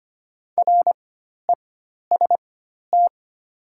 Morse code, heard as a keyed tone, about 25 wpm, spaced as for 12 wpm, 700 Hz. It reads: REST
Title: 4x Character Spacing - Top 500 Words - 25wpm